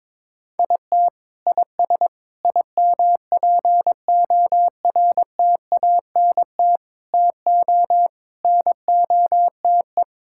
Morse code, heard as a keyed tone, about 22 wpm, 700 Hz. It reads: IT IS IMPORTANT TO NOTE
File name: practice / 22wpm / Sets of 6 Words - 22wpm